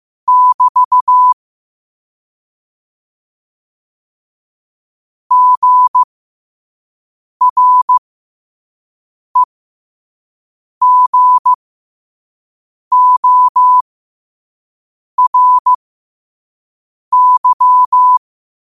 Morse code, heard as a keyed tone, about 15 wpm, spaced as for 5 wpm, 1000 Hz. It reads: <BT> GREGORY
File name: qotdfiles / 5wpm